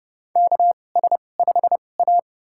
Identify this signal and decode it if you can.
Morse code, heard as a keyed tone, about 30 wpm, spaced as for 22 wpm, 700 Hz. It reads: KS5A
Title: USA Call Signs - 4 Characters - 2x Character Spacing - 30wpm